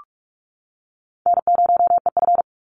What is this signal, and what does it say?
Morse code, heard as a keyed tone, about 48 wpm, 700 Hz. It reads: N0EF
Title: Easy Call Signs - 45wpm